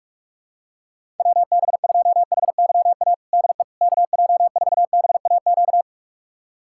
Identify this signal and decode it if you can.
Morse code, heard as a keyed tone, about 42 wpm, 700 Hz. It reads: WB2HYA DE KJ4BAQ
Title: USA Callsign DE Callsign - 6 Characters - 45wpm